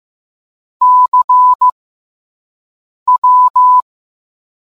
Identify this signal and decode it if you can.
Morse code, heard as a keyed tone, about 15 wpm, spaced as for 5 wpm, 1000 Hz. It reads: CW